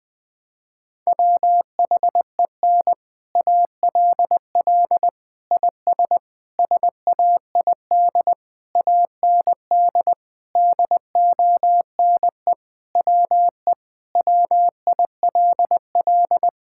Morse code, heard as a keyed tone, about 20 wpm, 700 Hz. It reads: WHEN ALL IS SAID AND DONE WE WILL
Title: Sentences with 6-Word Phrases - 20wpm